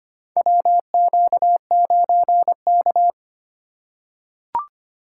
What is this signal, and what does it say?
Morse code, heard as a keyed tone, about 25 wpm, 700 Hz. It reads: WQ9K
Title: USA Call Signs - 4 Characters - 2x Character Spacing - 25wpm